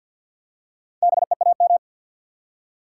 Morse code, heard as a keyed tone, about 53 wpm, 700 Hz. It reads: BEAM
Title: Top 90 QSO Elements 50wpm